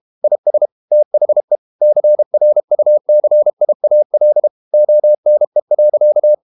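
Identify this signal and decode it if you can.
Morse code, heard as a keyed tone, about 32 wpm, 600 Hz. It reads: IS THE CRUCIAL ONE.
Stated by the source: Morse code practice at 32 wpm from CW med Gnister, reading Ken Follett's novel Never in English.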